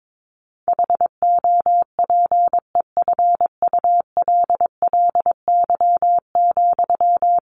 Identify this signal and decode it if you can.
Morse code, heard as a keyed tone, about 22 wpm, 700 Hz. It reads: HOPEFULLY,
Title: Sentences from Top 1500 Words 22wpm